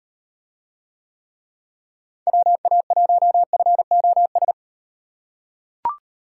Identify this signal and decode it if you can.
Morse code, heard as a keyed tone, about 37 wpm, 700 Hz. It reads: WA1FOS